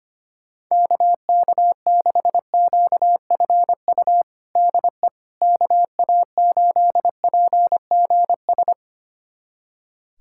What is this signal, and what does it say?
Morse code, heard as a keyed tone, about 25 wpm, 700 Hz. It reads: KK6QFU DE KA8PGS
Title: USA Callsign DE Callsign - 6 Characters - 25wpm